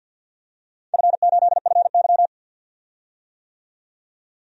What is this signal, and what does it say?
Morse code, heard as a keyed tone, about 51 wpm, 700 Hz. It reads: F8FY
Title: DXCC-100 Call Signs - 4 Characters - 50wpm